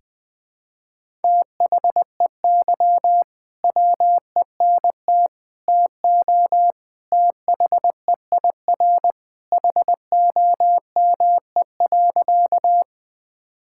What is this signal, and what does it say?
Morse code, heard as a keyed tone, about 20 wpm, 700 Hz. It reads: THEY WENT TO THEIR HOME.
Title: Sentences with 2-Word Phrases - Spoken Beginning - 20wpm